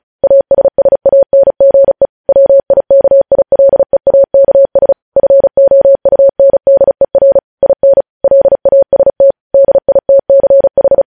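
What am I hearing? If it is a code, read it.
Morse code, heard as a keyed tone, about 36 wpm, 550 Hz. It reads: ASSANGE WIKILEAKS FOUNDER IN LAST DITCH
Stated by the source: News headlines sent in Morse at 35 wpm.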